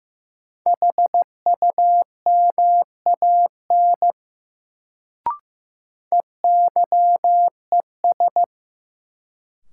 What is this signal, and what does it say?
Morse code, heard as a keyed tone, about 15 wpm, 700 Hz. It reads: HUMAN EYES
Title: Sets of 5 Words - New Word Drill - 15wpm